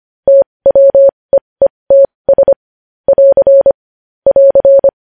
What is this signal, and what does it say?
Morse code, heard as a keyed tone, about 25 wpm, spaced as for 20 wpm, 550 Hz. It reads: TWEETS <AR> <AR>